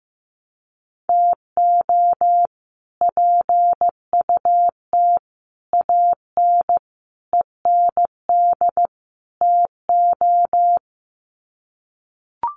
Morse code, heard as a keyed tone, about 15 wpm, 700 Hz. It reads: TO PUT AN END TO
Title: Sets of 5 Words Encore2 15wpm